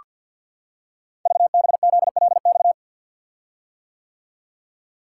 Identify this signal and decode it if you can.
Morse code, heard as a keyed tone, about 48 wpm, 690 Hz. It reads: VBZLX
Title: Code Groups - 5 Letters - 50wpm